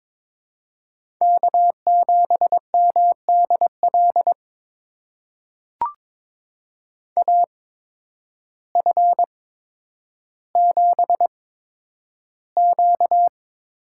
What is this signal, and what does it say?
Morse code, heard as a keyed tone, about 22 wpm, spaced as for 10 wpm, 700 Hz. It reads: K7MDL AF7Q
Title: USA Call Signs - 5 Characters - 8x Character Spacing - 22wpm